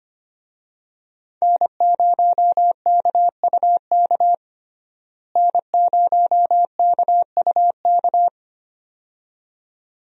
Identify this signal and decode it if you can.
Morse code, heard as a keyed tone, about 25 wpm, 700 Hz. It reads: N0KUK N0KUK
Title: USA Call Signs - 5 Characters - Repeated 2x - 25wpm